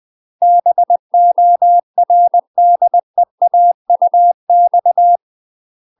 Morse code, heard as a keyed tone, about 20 wpm, 700 Hz. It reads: BORDEAUX
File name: UK MASTER QSOs TRAINING / 20WPM / UK MASTER QSOs 0052